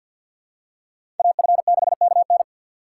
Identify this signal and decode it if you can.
Morse code, heard as a keyed tone, about 52 wpm, 700 Hz. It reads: AF6KN
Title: USA Call Signs - 5 Characters - 4x Character Spacing - 50wpm